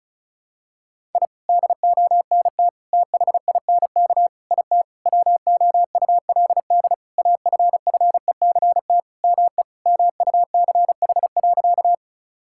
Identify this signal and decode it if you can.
Morse code, heard as a keyed tone, about 35 wpm, 700 Hz. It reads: I DONT THINK IT WOULD AFFECT ME MUCH.
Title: Sentences from Top 1000 Words - Review Part 6 - 35wpm